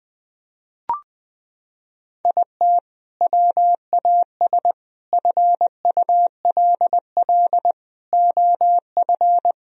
Morse code, heard as a keyed tone, about 21 wpm, 700 Hz. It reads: IT WAS FULL OF
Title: Sentences from Top 1000 Words - Review Part 3 - 20wpm